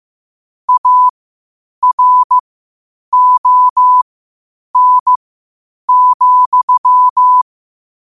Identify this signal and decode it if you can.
Morse code, heard as a keyed tone, about 15 wpm, spaced as for 8 wpm, 1000 Hz. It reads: ARON,
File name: qotdfiles / 8wpm